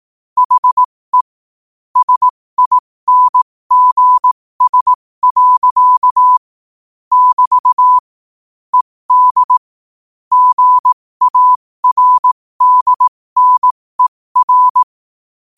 Morse code, heard as a keyed tone, about 18 wpm, spaced as for 15 wpm, 1000 Hz. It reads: HE SINGS. <BT> ED GARDNER
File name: qotdfiles / 15wpm